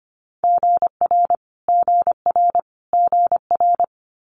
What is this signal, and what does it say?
Morse code, heard as a keyed tone, about 25 wpm, 700 Hz. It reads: GR GR GR
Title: Multi-letter Phonemes Repeated 3x 25wpm